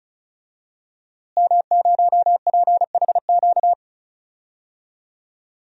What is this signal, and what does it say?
Morse code, heard as a keyed tone, about 34 wpm, 700 Hz. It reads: M0PHQ
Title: DXCC-100 Call Signs - 5 Characters - Repeated 3x - 35wpm